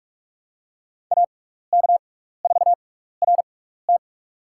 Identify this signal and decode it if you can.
Morse code, heard as a keyed tone, about 47 wpm, spaced as for 16 wpm, 700 Hz. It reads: AK4RT
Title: USA Call Signs - 5 Characters - 6x Character Spacing - 45wpm